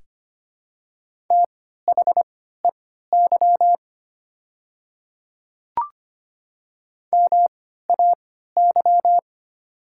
Morse code, heard as a keyed tone, about 25 wpm, spaced as for 14 wpm, 700 Hz. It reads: THEY MAY
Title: 3x Character Spacing - Top 100 Words - 25wpm